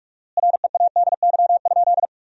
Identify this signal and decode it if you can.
Morse code, heard as a keyed tone, about 46 wpm, 700 Hz. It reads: READY?
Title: 4x Word-Spacing - Sentences from Top 400 Words - 45wpm